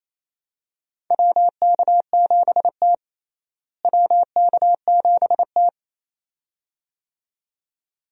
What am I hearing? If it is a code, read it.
Morse code, heard as a keyed tone, about 28 wpm, 700 Hz. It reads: WK7T WK7T
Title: USA Call Signs - 4 Characters - Repeated 2x - 28wpm